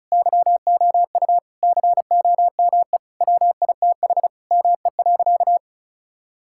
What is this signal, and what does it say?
Morse code, heard as a keyed tone, about 36 wpm, 700 Hz. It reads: YOU COME WITH ME.
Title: Sentences from Top 1000 Words - Review Part 6 - 35wpm